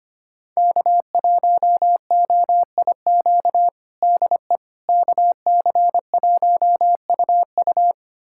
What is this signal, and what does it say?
Morse code, heard as a keyed tone, about 25 wpm, 700 Hz. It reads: K1OIQ DE KC1UU
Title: USA Callsign DE Callsign - 5 Characters - 25wpm